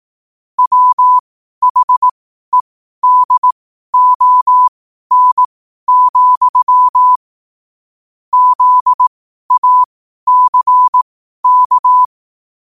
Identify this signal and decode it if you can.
Morse code, heard as a keyed tone, about 18 wpm, spaced as for 12 wpm, 1000 Hz. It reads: WHEDON, ZACK
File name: qotdfiles / 12wpm